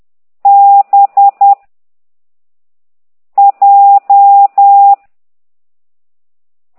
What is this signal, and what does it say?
Morse code, heard as a keyed tone, about 10 wpm, 800 Hz. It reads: B J
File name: ct2idw / exerc9